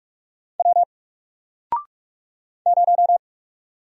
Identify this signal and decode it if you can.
Morse code, heard as a keyed tone, about 44 wpm, 700 Hz. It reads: W 0
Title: Single Character (Letter:Number:Punctuation:Prosign) Alt 45wpm